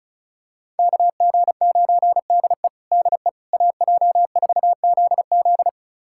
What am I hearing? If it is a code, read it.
Morse code, heard as a keyed tone, about 34 wpm, 700 Hz. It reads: KG9DE DE AJ4ZZ
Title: USA Callsign DE Callsign - 5 Characters - 35wpm